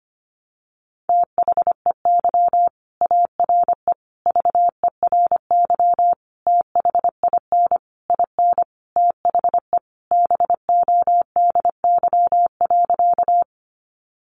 Morse code, heard as a keyed tone, about 25 wpm, 700 Hz. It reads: THEY ARE VERY THIN IN THE BODY.
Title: Sentences from Top 1200 Words 25wpm